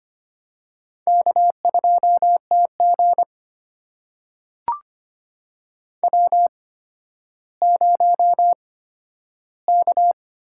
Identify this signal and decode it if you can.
Morse code, heard as a keyed tone, about 25 wpm, spaced as for 12 wpm, 700 Hz. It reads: K2TG W0K
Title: USA Call Signs - 8x Character Spacing - 25wpm